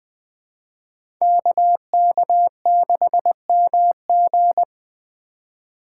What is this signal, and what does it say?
Morse code, heard as a keyed tone, about 20 wpm, 700 Hz. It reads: KK6MG
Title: USA Call Signs - 5 Characters - 3x Character Spacing - 20wpm